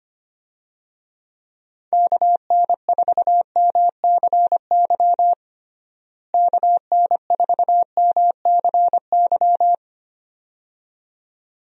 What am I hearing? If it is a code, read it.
Morse code, heard as a keyed tone, about 25 wpm, 700 Hz. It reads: KN4MCY KN4MCY
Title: USA Call Signs - 6 Characters - Repeated 2x - 25wpm